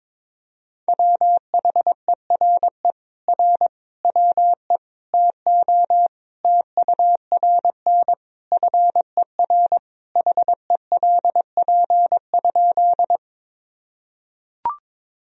Morse code, heard as a keyed tone, about 22 wpm, 700 Hz. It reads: WHERE R WE TO TURN FER HELP?
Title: Sentences from Top 200 Words and Abbreviations 22wpm